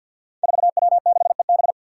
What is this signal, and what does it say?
Morse code, heard as a keyed tone, about 48 wpm, 700 Hz. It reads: 4W6EB
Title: Five Challenging Letter-Number Combinations 50wpm